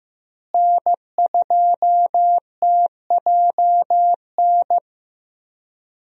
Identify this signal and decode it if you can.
Morse code, heard as a keyed tone, about 15 wpm, 700 Hz. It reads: N2TJN
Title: USA Call Signs - 5 Characters - Repeated 3x - 15wpm